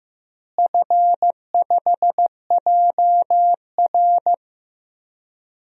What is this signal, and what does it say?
Morse code, heard as a keyed tone, about 16 wpm, 700 Hz. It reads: F5JR
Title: DXCC-100 Call Signs - 4 Characters - Repeated 3x - 15wpm